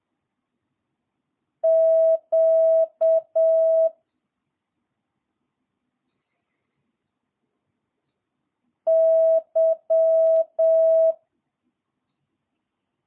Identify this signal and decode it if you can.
Morse code, heard as a keyed tone, about 7 wpm, 650 Hz. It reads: Q Y
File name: jh8cbh / 2022cw